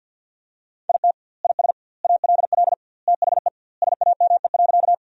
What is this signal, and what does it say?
Morse code, heard as a keyed tone, about 51 wpm, 700 Hz. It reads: IT IS ALL THE SAME.
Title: 2x Word-Spacing - Sentences from Top 200 Words - 50wpm